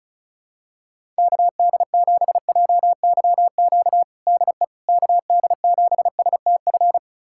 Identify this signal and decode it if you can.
Morse code, heard as a keyed tone, about 36 wpm, 700 Hz. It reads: KD7JYQ DE KD7STF
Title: USA Callsign DE Callsign 35wpm